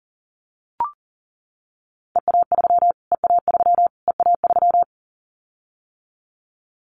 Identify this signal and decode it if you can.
Morse code, heard as a keyed tone, about 37 wpm, 700 Hz. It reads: EA3 EA3 EA3
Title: Callsign Prefixes for Top 25 DXCC - Repeated 3x - 40wpm